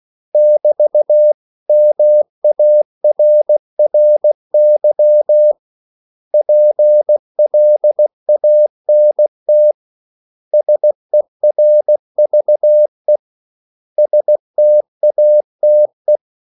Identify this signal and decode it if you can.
Morse code, heard as a keyed tone, about 16 wpm, 600 Hz. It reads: <BT> MARRY PLANT SERVE STATE
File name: CW / Practice / 500MostCommonEnglishWords16